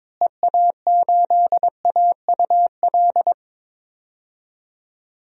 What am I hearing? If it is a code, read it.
Morse code, heard as a keyed tone, about 22 wpm, 700 Hz. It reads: EA8AUL
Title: DXCC-100 Call Signs - 6 Characters - Repeated 3x - 22wpm